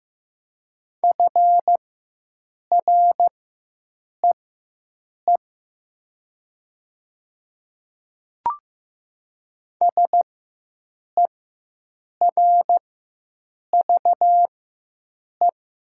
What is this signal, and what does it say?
Morse code, heard as a keyed tone, about 15 wpm, spaced as for 7 wpm, 700 Hz. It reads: FREE SERVE
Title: 4x Character Spacing - Top 500 Words - 15wpm